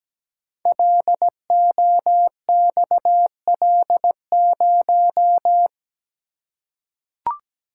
Morse code, heard as a keyed tone, about 17 wpm, 690 Hz. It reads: LOXL0
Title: Code Groups - 5 Letter:Numbers - 17wpm